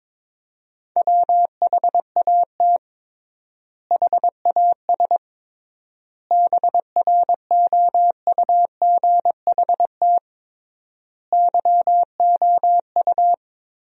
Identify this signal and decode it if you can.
Morse code, heard as a keyed tone, about 22 wpm, 700 Hz. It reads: WHAT HAS BROUGHT YOU
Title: 3x Word-Spacing - Sentences from Top 500 Words - 22wpm